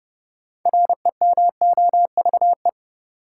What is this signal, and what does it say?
Morse code, heard as a keyed tone, about 30 wpm, 700 Hz. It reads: REMOVE
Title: Sentences from Top 800 Words - New Word Drill - 30wpm